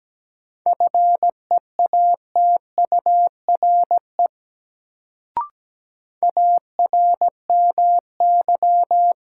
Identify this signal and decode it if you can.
Morse code, heard as a keyed tone, about 17 wpm, 700 Hz. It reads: FEATURE ARMY
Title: Sentences from Top 800 Words - New Word Drill - 17wpm